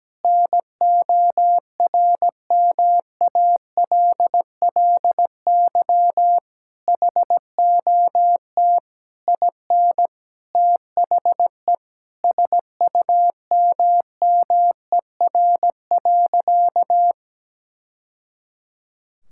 Morse code, heard as a keyed tone, about 17 wpm, 700 Hz. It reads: NORMALLY HOT IN THE SUMMER.
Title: Sentences from Top 1000 Words - Review Part 6 - 17wpm